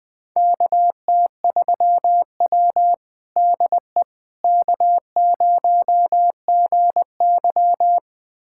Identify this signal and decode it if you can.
Morse code, heard as a keyed tone, about 21 wpm, 700 Hz. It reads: KT3W DE K0GY
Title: USA Callsign DE Callsign - 4 Characters - 20wpm